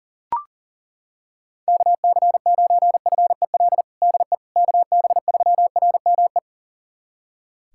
Morse code, heard as a keyed tone, about 37 wpm, 700 Hz. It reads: KC9FEL DE KB3RME
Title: USA Callsign DE Callsign - 6 Characters - 40wpm